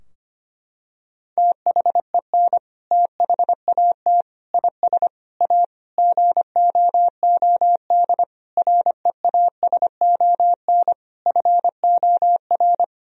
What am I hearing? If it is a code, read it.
Morse code, heard as a keyed tone, about 25 wpm, 700 Hz. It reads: THEN THAT IS A GOOD REASON FOR
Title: Sentences from Top 1000 Words - Review Part 4 - 25wpm